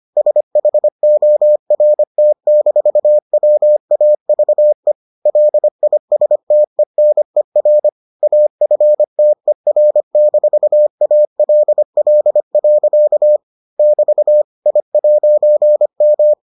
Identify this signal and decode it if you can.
Morse code, heard as a keyed tone, about 25 wpm, 600 Hz. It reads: SHORT-WAVE LISTENER AFTER-ALL. <BT> I'M